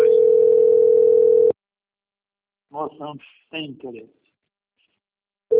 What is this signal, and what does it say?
Ringback tone (North America), 440 + 480 Hz, with speech.
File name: out- - - - - .10990